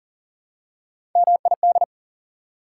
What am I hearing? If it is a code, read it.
Morse code, heard as a keyed tone, about 40 wpm, 700 Hz. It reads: MID
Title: Prefixes and Suffixes 40wpm